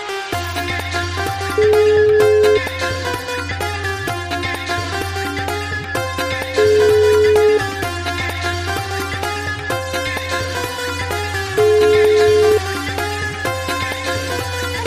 Ringback tone (Europe, 425 Hz), with noise.